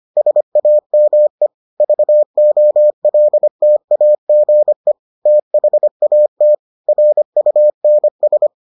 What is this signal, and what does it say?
Morse code, heard as a keyed tone, about 25 wpm, 600 Hz. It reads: SAME VOLTAGE THAT RUNS